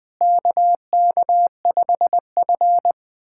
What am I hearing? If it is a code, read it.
Morse code, heard as a keyed tone, about 21 wpm, 700 Hz. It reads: KK5F
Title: USA Callsign DE Callsign - 4 Characters - 20wpm